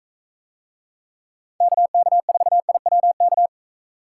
Morse code, heard as a keyed tone, about 44 wpm, 700 Hz. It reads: KK4IWK